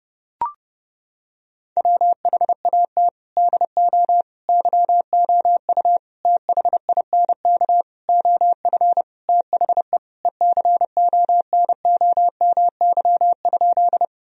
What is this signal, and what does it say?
Morse code, heard as a keyed tone, about 30 wpm, 700 Hz. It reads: WHAT DO YOU THINK OF THE ECONOMY?
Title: Sentences from Top 1000 Words - Review Part 3 - 30wpm